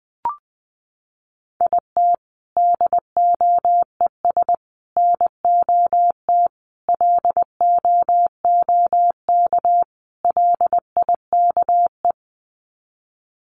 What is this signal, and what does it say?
Morse code, heard as a keyed tone, about 20 wpm, 700 Hz. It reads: IT DOES NOT LOOK LIKE
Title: Sets of 5 Words Encore2 20wpm